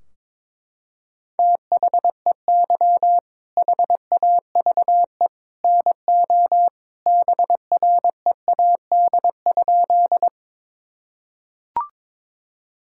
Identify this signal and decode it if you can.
Morse code, heard as a keyed tone, about 21 wpm, 700 Hz. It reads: THEY HAVE NO BREAD?
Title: Sentences from Top 1600 Words 22wpm